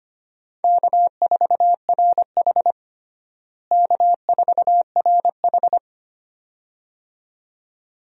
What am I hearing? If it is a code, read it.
Morse code, heard as a keyed tone, about 25 wpm, 700 Hz. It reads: K4RH K4RH
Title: USA Call Signs - 4 Characters - Repeated 2x - 25wpm